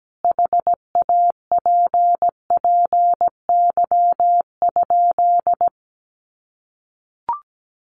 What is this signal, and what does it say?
Morse code, heard as a keyed tone, about 17 wpm, 700 Hz. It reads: HAPPY?
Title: Sentences from Top 1000 Words 17wpm